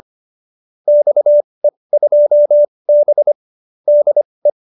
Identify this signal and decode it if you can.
Morse code, heard as a keyed tone, about 25 wpm, spaced as for 20 wpm, 590 Hz. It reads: XE2B DE